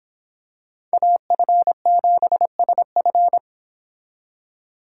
Morse code, heard as a keyed tone, about 26 wpm, 700 Hz. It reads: AF7SF